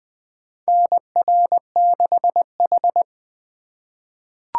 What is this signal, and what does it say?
Morse code, heard as a keyed tone, about 20 wpm, 700 Hz. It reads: NR6H
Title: USA Call Signs - 4x Character Spacing - 20wpm